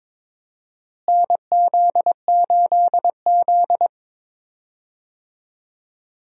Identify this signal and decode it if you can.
Morse code, heard as a keyed tone, about 22 wpm, 700 Hz. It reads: NZ8Z
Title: USA Call Signs - 4 Characters - Repeated 2x - 22wpm